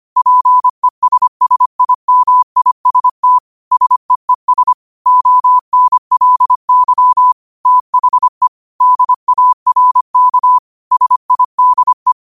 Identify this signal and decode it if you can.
Morse code, heard as a keyed tone, about 25 wpm, 1000 Hz. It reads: PESSIMIST SEES ONLY THE DARK SIDE
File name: qotdfiles / 25wpm